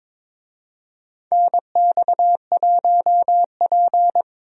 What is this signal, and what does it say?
Morse code, heard as a keyed tone, about 22 wpm, 700 Hz. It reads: NX1P
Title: USA Call Signs - 4 Characters - 22wpm